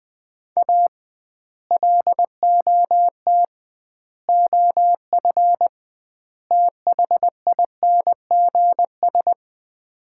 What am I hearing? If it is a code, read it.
Morse code, heard as a keyed tone, about 20 wpm, 700 Hz. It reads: A LOT OF THINGS
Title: 2x Word-Spacing - Sets of 4 Words - 20wpm